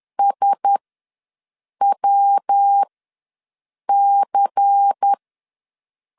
Morse code, heard as a keyed tone, about 10 wpm, spaced as for 6 wpm, 800 Hz. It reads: SWC
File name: LEZIONI / 8 YJ45 / 8 YJ4 ws 10wpm 4020